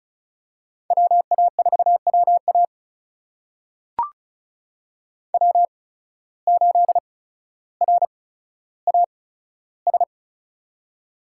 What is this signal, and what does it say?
Morse code, heard as a keyed tone, about 34 wpm, spaced as for 15 wpm, 700 Hz. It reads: WA4WA W8RAS
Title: USA Call Signs - 5 Characters - 8x Character Spacing - 35wpm